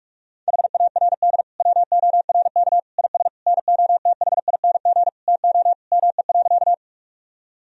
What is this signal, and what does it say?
Morse code, heard as a keyed tone, about 46 wpm, 700 Hz. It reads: HARD WORK IS NOTHING TO ME.
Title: Sentences from Top 1000 Words - Review Part 1 - 45wpm